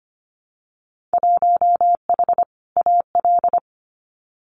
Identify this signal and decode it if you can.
Morse code, heard as a keyed tone, about 25 wpm, 700 Hz. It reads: 1H AL
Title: Winter Field Day Exchanges 25wpm